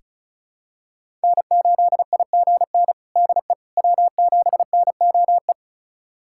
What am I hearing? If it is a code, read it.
Morse code, heard as a keyed tone, about 34 wpm, 700 Hz. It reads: N8IGN DE W7NOE
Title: USA Callsign DE Callsign - 5 Characters - 35wpm